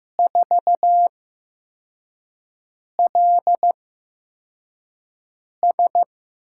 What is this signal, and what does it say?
Morse code, heard as a keyed tone, about 15 wpm, spaced as for 4 wpm, 700 Hz. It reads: 4LS
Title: USA Call Signs - 8x Character Spacing - 15wpm